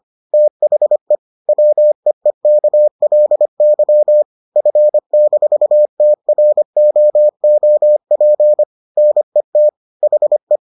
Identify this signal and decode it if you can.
Morse code, heard as a keyed tone, about 25 wpm, 600 Hz. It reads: THE WEEKLY F-TROOP NET HE